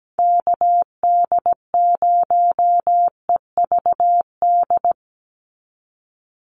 Morse code, heard as a keyed tone, about 17 wpm, 700 Hz. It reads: KD0EVD
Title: USA Call Signs 17wpm